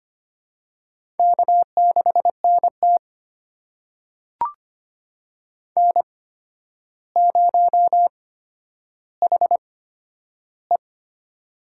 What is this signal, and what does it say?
Morse code, heard as a keyed tone, about 25 wpm, spaced as for 11 wpm, 700 Hz. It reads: K6NT N0HE
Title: USA Call Signs - 4 Characters - 8x Character Spacing - 25wpm